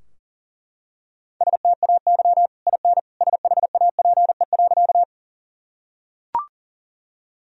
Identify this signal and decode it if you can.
Morse code, heard as a keyed tone, about 40 wpm, 700 Hz. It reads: STAY IN SHAPE.
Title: Sentences from Top 1000 Words - Review Part 3 - 40wpm